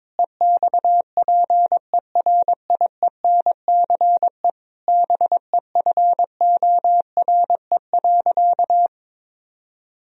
Morse code, heard as a keyed tone, about 22 wpm, 700 Hz. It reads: EXPERIENCE BEFORE.
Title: Sentences from Top 1000 Words - Review Part 3 - 22wpm